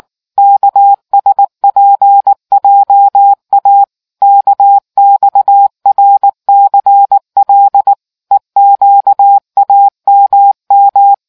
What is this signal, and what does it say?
Morse code, heard as a keyed tone, about 19 wpm, 800 Hz. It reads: KSPJA KXRCL EQAMM